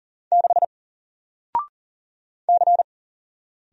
Morse code, heard as a keyed tone, about 40 wpm, 700 Hz. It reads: 6 C
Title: Single Letter-Number Alt 40wpm